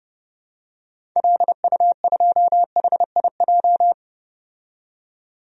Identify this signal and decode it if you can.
Morse code, heard as a keyed tone, about 30 wpm, 700 Hz. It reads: LU2HIJ
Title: DXCC-100 Call Signs - 6 Characters - 30wpm